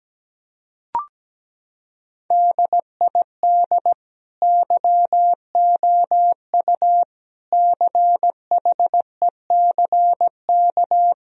Morse code, heard as a keyed tone, about 17 wpm, 700 Hz. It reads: DID YOU CHECK
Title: Sentences from Top 1000 Words - Review Part 3 - 17wpm